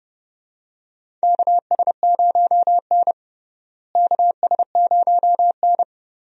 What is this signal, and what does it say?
Morse code, heard as a keyed tone, about 30 wpm, 700 Hz. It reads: KS0N KS0N
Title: USA Call Signs - 4 Characters - Repeated 2x - 30wpm